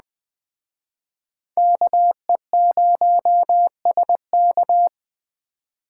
Morse code, heard as a keyed tone, about 21 wpm, 700 Hz. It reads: KE0SK